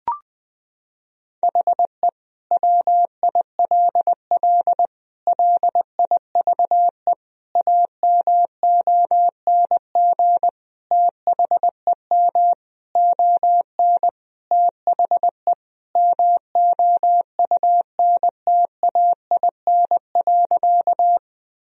Morse code, heard as a keyed tone, about 20 wpm, 700 Hz. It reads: HE WILL LIVE AMONG THEM ON THE MOUNTAIN.
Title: Sentences from Top 1000 Words - Review Part 3 - 20wpm